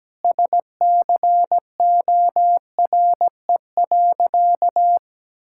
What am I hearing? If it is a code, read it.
Morse code, heard as a keyed tone, about 17 wpm, 700 Hz. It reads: SCORE.
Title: Sentences from Top 1600 Words 17wpm